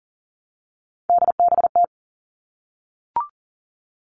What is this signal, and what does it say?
Morse code, heard as a keyed tone, about 40 wpm, 700 Hz. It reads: DBT
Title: All Cut Numbers - 3-Digit 40wpm